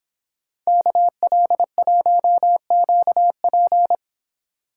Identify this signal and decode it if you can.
Morse code, heard as a keyed tone, about 26 wpm, 700 Hz. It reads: KL1QP